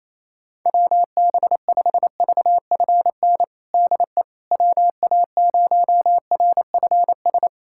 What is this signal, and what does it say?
Morse code, heard as a keyed tone, about 28 wpm, 700 Hz. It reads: WB5VFN DE WA0RFS
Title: USA Callsign DE Callsign - 6 Characters - 28wpm